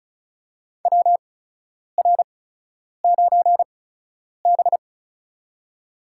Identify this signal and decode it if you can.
Morse code, heard as a keyed tone, about 34 wpm, spaced as for 10 wpm, 700 Hz. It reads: WR9B
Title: USA Call Signs - 4 Characters - 8x Character Spacing - 35wpm